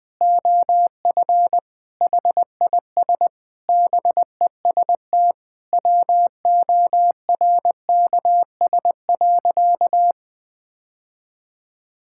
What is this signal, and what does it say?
Morse code, heard as a keyed tone, about 21 wpm, 700 Hz. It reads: OF HIS BEST WORKS.
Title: Sentences with 3-Word Phrases - 20wpm